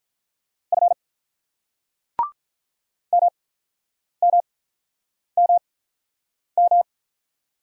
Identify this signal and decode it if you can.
Morse code, heard as a keyed tone, about 44 wpm, spaced as for 9 wpm, 700 Hz. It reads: F MMMM